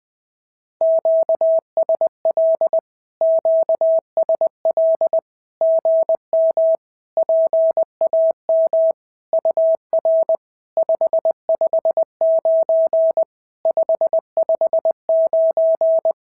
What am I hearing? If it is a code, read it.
Morse code, heard as a keyed tone, about 20 wpm, 690 Hz. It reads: QSL QSL GM PAM UR 559 559
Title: SOTA QSO Practice 20wpm